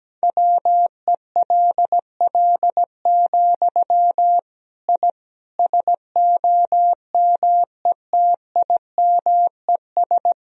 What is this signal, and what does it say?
Morse code, heard as a keyed tone, about 17 wpm, 700 Hz. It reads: WELL, I SOMETIMES
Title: Sentences from Top 1000 Words - Review Part 3 - 17wpm